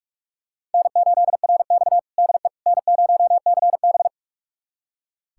Morse code, heard as a keyed tone, about 46 wpm, 700 Hz. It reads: N8RX DE N0CB
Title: USA Callsign DE Callsign - 4 Characters - 45wpm